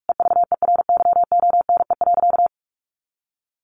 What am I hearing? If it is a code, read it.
Morse code, heard as a keyed tone, about 44 wpm, 700 Hz. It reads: EVERYONE.
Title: A Sound Of Thunder 45wpm